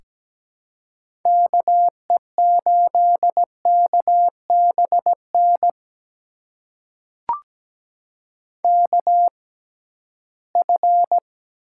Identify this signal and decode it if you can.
Morse code, heard as a keyed tone, about 17 wpm, spaced as for 13 wpm, 700 Hz. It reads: KE8KBN KF
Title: USA Call Signs - 6 Characters - 6x Character Spacing - 17wpm